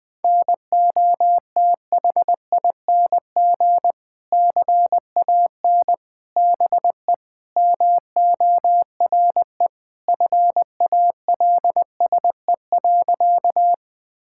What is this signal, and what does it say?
Morse code, heard as a keyed tone, about 20 wpm, 700 Hz. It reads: NOTHING CAN BE MORE FALSE.
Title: Sentences from Top 1600 Words 20wpm